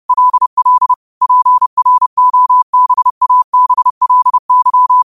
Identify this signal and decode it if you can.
Morse code, heard as a keyed tone, about 30 wpm, 1000 Hz. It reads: LL PROBABLY
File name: qotdfiles / 30wpm